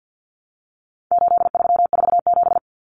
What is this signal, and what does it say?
Morse code, heard as a keyed tone, about 50 wpm, 700 Hz. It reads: 8347
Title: Numbers - 4-Digit 50wpm